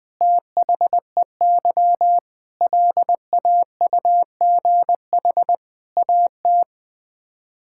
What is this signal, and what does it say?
Morse code, heard as a keyed tone, about 20 wpm, 700 Hz. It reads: THEY LAUGH AT
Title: Sets of 8 Words - 20wpm